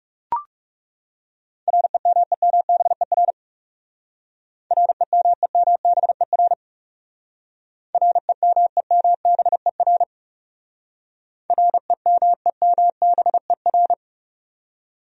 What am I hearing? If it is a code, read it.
Morse code, heard as a keyed tone, about 36 wpm, 700 Hz. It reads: REMEMBER REMEMBER REMEMBER REMEMBER
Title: Speed-Racing Top 500 Words 30wpm